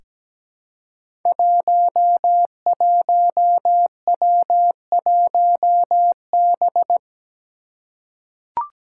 Morse code, heard as a keyed tone, about 17 wpm, 700 Hz. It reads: 11W1B
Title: Five Challenging Letter-Number Combinations 17wpm